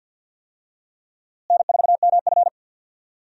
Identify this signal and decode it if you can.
Morse code, heard as a keyed tone, about 51 wpm, 700 Hz. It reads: N4MF
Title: USA Call Signs - 4 Characters - Repeated 3x - 50wpm